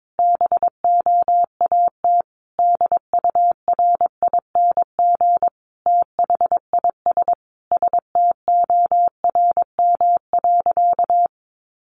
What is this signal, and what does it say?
Morse code, heard as a keyed tone, about 22 wpm, 700 Hz. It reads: BOAT DURING THIS STORM.
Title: Sentences from Top 1500 Words 22wpm